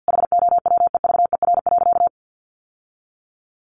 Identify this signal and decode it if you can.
Morse code, heard as a keyed tone, about 50 wpm, 700 Hz. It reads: HOWEVER.
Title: Sentences from Top 600 Words 50wpm